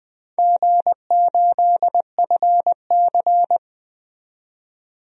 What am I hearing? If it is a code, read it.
Morse code, heard as a keyed tone, about 20 wpm, 700 Hz. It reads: G8FC
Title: DXCC-100 Call Signs - 4 Characters - 20wpm